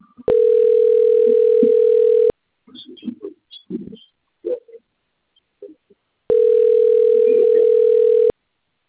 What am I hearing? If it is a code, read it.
Ringback tone (North America), 440 + 480 Hz.
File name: out- - - - - .65184